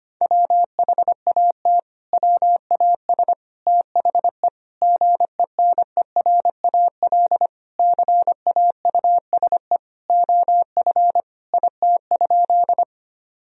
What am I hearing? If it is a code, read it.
Morse code, heard as a keyed tone, about 25 wpm, 700 Hz. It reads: WHAT WAS THE GENERAL CAUSE OF IT?
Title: Sentences from Top 1000 Words - Review Part 3 - 25wpm